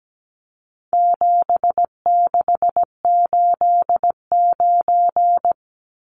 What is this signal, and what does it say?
Morse code, heard as a keyed tone, about 17 wpm, 700 Hz. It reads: 7689
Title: Numbers - 4-Digit 17wpm